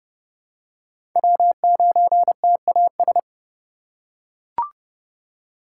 Morse code, heard as a keyed tone, about 30 wpm, 700 Hz. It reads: W9TAS
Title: USA Call Signs - 5 Characters - 3x Character Spacing - 30wpm